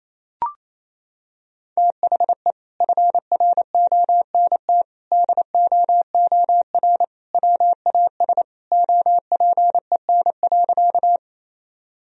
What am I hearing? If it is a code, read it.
Morse code, heard as a keyed tone, about 28 wpm, 700 Hz. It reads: THE FRONT DOOR WAS OPEN.
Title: Sentences from Top 1000 Words - Review Part 2 - 28wpm